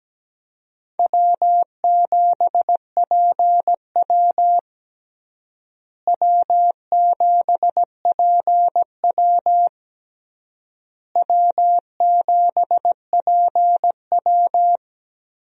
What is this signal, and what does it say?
Morse code, heard as a keyed tone, about 17 wpm, 700 Hz. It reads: W7PW W7PW W7PW
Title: USA Call Signs - 4 Characters - Repeated 3x - 17wpm